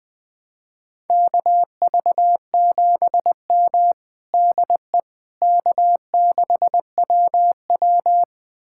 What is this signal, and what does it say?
Morse code, heard as a keyed tone, about 20 wpm, 700 Hz. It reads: KV7M DE K6WW
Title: USA Callsign DE Callsign - 4 Characters - 20wpm